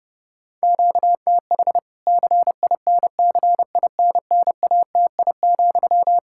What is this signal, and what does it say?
Morse code, heard as a keyed tone, about 30 wpm, 700 Hz. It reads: QTH CINCINNATI,
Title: K1USN SST QSO Practice 30wpm